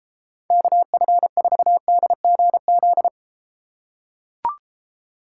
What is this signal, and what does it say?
Morse code, heard as a keyed tone, about 34 wpm, 700 Hz. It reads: KF4DGZ